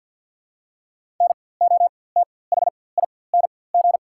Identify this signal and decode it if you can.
Morse code, heard as a keyed tone, about 49 wpm, spaced as for 23 wpm, 690 Hz. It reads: NOTHING
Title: 4x Character Spacing - Top 500 Words - 50wpm